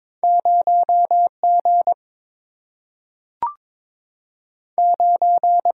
Morse code, heard as a keyed tone, about 22 wpm, 700 Hz. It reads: 0 G 9
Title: Single-Letter Single-Number 22wpm